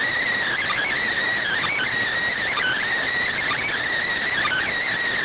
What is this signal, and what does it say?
Emergency Alert System (SAME) tones (header bursts), heard as harsh buzzing bursts. It is heard from 0.5 to 4.8 s.